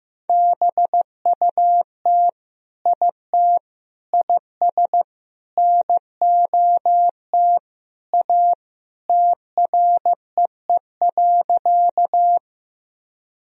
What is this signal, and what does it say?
Morse code, heard as a keyed tone, about 16 wpm, 700 Hz. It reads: BUT IT IS NOT A TREE.
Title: Sentences from Top 1000 Words - Review Part 1 - 15wpm